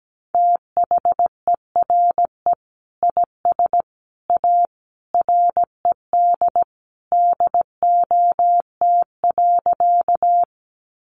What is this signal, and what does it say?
Morse code, heard as a keyed tone, about 17 wpm, 700 Hz. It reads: THERE IS A RED DOT.
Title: Sentences from Top 1500 Words 17wpm